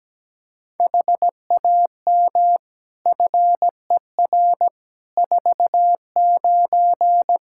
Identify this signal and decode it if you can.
Morse code, heard as a keyed tone, about 17 wpm, 700 Hz. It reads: HAM FER 49
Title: Practice QSO - Exchange 3 - More Info - 17wpm